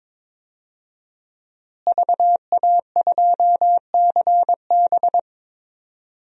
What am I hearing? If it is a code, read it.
Morse code, heard as a keyed tone, about 22 wpm, 700 Hz. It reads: VA2CB
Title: DXCC-100 Call Signs - 5 Characters - Repeated 2x - 22wpm